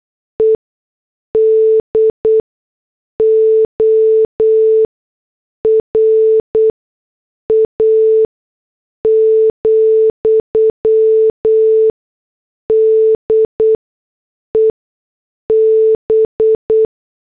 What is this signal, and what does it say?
Morse code, heard as a keyed tone, about 8 wpm, spaced as for 6 wpm, 440 Hz. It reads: EDORA, DEB
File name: morse / headlines